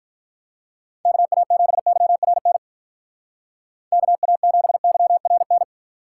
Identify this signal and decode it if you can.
Morse code, heard as a keyed tone, about 53 wpm, 700 Hz. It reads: KA7YRN KA7YRN